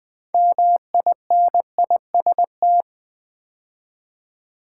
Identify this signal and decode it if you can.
Morse code, heard as a keyed tone, about 20 wpm, 700 Hz. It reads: MINIST
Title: Build A Word - 2 Letters At A Time - 8-Letter Words - 20wpm